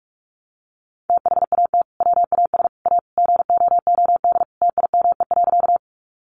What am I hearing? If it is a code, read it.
Morse code, heard as a keyed tone, about 44 wpm, 700 Hz. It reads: THAT WAS A GOOD TIME.
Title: Sentences from Top 200 Words 45wpm